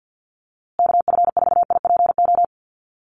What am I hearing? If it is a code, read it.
Morse code, heard as a keyed tone, about 50 wpm, 700 Hz. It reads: KF4IPO
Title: USA Call Signs Repeated 2x 50wpm